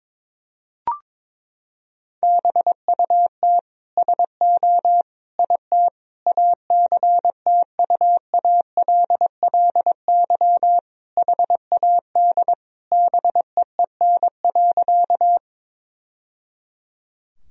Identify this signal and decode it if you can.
Morse code, heard as a keyed tone, about 22 wpm, 700 Hz. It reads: BUT SO IT ACTUALLY HAD BEEN.
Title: Sentences from Top 1000 Words - Review Part 4 - 22wpm